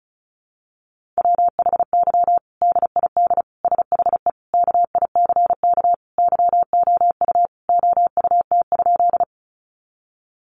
Morse code, heard as a keyed tone, about 34 wpm, 700 Hz. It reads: WHY DID SHE KICK YOU OUT?
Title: Sentences from Top 1300 Words 35wpm